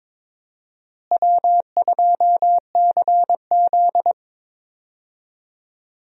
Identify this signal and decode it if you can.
Morse code, heard as a keyed tone, about 22 wpm, 700 Hz. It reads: W2CZ
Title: USA Call Signs - 4 Characters - Repeated 2x - 22wpm